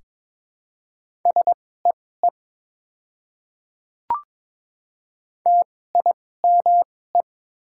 Morse code, heard as a keyed tone, about 21 wpm, spaced as for 16 wpm, 690 Hz. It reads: SEE TIME
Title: 2x Character Spacing - Top 300 Words - 22wpm